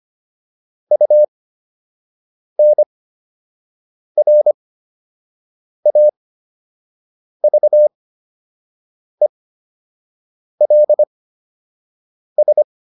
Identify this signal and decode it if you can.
Morse code, heard as a keyed tone, about 25 wpm, spaced as for 6 wpm, 600 Hz. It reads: UNRAVELS